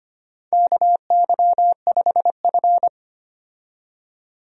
Morse code, heard as a keyed tone, about 25 wpm, 700 Hz. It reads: KY5F
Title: USA Call Signs - 4 Characters - 25wpm